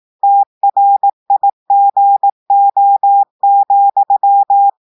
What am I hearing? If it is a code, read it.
Morse code, heard as a keyed tone, about 18 wpm, 800 Hz. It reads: TRIGO,
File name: audios / portugais / portugais texte en clair 1 vitesse 18 mots minutes